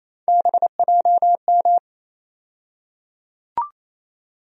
Morse code, heard as a keyed tone, about 28 wpm, 700 Hz. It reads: BJM
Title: USA Call Sign Postfixes - 3 Letters 28wpm